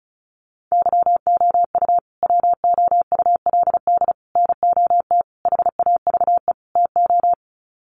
Morse code, heard as a keyed tone, about 34 wpm, 700 Hz. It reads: YOU WOULD NOT HAVE TO
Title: Sets of 5 Words Encore 35wpm